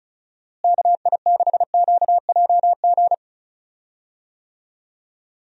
Morse code, heard as a keyed tone, about 36 wpm, 700 Hz. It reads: KI6QJG
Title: USA Call Signs - 6 Characters - Repeated 2x - 35wpm